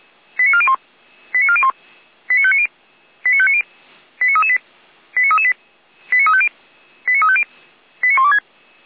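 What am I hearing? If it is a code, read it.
Five-tone selective call (ZVEI), heard as quick runs of five tones, tones of about 70 ms. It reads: 89401 89401 89590 89590 89338 89338 89350 89350 89126
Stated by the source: ZVEI five-tone call received with an SDR.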